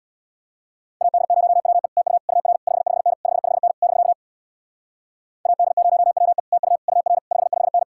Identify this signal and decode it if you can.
Morse code, heard as a keyed tone, about 62 wpm, 700 Hz. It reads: AD0WE TU UR 55N 55N <BK> AD0WE TU UR 55N
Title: Speed-Racing Common QSO Phrases - 50wpm